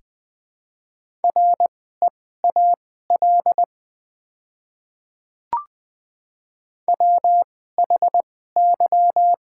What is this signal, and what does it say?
Morse code, heard as a keyed tone, about 20 wpm, spaced as for 14 wpm, 700 Hz. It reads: REAL WHY
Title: 2x Character Spacing - Top 500 Words - 20wpm